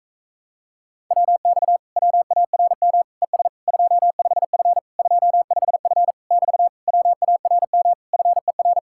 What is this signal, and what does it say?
Morse code, heard as a keyed tone, about 46 wpm, 700 Hz. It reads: WX WARM ES 25F 25F <BT> WARM FER